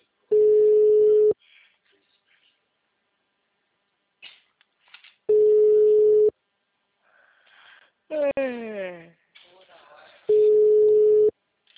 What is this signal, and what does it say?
Ringback tone (Europe, 425 Hz), with speech.